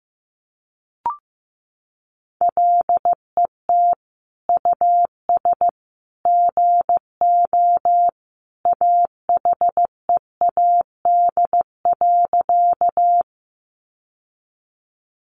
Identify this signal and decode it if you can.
Morse code, heard as a keyed tone, about 15 wpm, 700 Hz. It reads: LET US GO AHEAD.
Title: Sentences from Top 1100 Words 15wpm